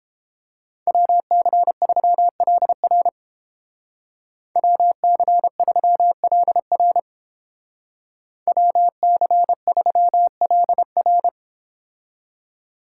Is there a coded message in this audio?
Yes: Morse code from 0.9 to 11.3 s, heard as a keyed tone, about 30 wpm, 700 Hz. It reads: WC3LR WC3LR WC3LR